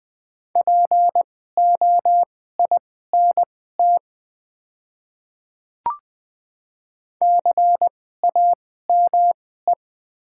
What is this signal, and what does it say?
Morse code, heard as a keyed tone, about 20 wpm, spaced as for 14 wpm, 700 Hz. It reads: POINT CAME
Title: 2x Character Spacing - Top 200 Words - 20wpm